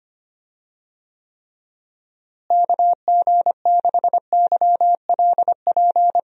Morse code, heard as a keyed tone, about 25 wpm, 700 Hz. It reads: KG6YLP
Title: USA Call Signs - 6 Characters - Repeated 3x - 25wpm